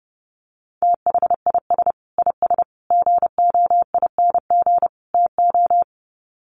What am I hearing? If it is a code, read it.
Morse code, heard as a keyed tone, about 30 wpm, 700 Hz. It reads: THIS IS GOING TO
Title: Sets of 4 Words 30wpm